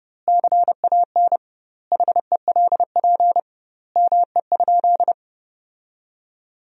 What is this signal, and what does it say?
Morse code, heard as a keyed tone, about 32 wpm, 700 Hz. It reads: CAN HELP ME?
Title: 2x Word-Spacing - Sentences from Top 200 Words - 30wpm